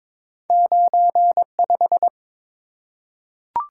Morse code, heard as a keyed tone, about 23 wpm, 700 Hz. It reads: 95
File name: practice / 22wpm / Two Letter-Number Combinations - 22wpm